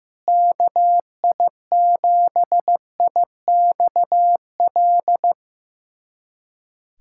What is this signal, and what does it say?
Morse code, heard as a keyed tone, about 15 wpm, 700 Hz. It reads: KI7IXL
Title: USA Callsign DE Callsign - 6 Characters - 15wpm